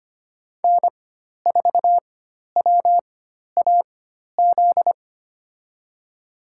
Morse code, heard as a keyed tone, about 25 wpm, spaced as for 12 wpm, 700 Hz. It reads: N4WAZ
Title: USA Call Signs - 5 Characters - 4x Character Spacing - 25wpm